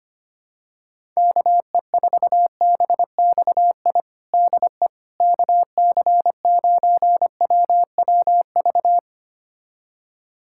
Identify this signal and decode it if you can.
Morse code, heard as a keyed tone, about 25 wpm, 700 Hz. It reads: KE4BXI DE KC9WWV
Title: USA Callsign DE Callsign 25wpm